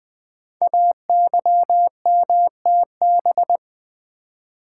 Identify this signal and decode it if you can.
Morse code, heard as a keyed tone, about 20 wpm, 700 Hz. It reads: AYMTB
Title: Code Groups - 5 Letters - 20wpm